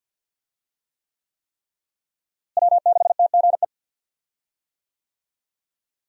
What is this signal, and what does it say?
Morse code, heard as a keyed tone, about 51 wpm, 700 Hz. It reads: W6TGE
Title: USA Call Signs - 5 Characters - Repeated 3x - 50wpm